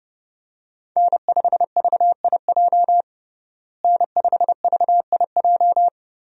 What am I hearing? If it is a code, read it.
Morse code, heard as a keyed tone, about 32 wpm, 700 Hz. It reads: N5VIJ N5VIJ
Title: USA Call Signs - 5 Characters - Repeated 2x - 30wpm